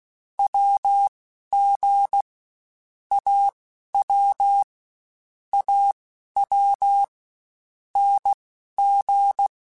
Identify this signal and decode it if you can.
Morse code, heard as a keyed tone, about 16 wpm, spaced as for 11 wpm, 800 Hz. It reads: WG AW AW NG
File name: Short Runs / Short Runs-mp3s / Unballanced @15